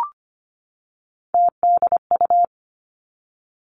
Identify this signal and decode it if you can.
Morse code, heard as a keyed tone, about 25 wpm, 700 Hz. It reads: TDU
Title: Three Challenging Letter-Number Combinations 25wpm